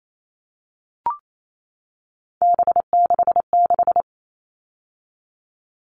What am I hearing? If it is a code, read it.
Morse code, heard as a keyed tone, about 28 wpm, 700 Hz. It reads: B66
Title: All Cut Numbers - 3-Digit 28wpm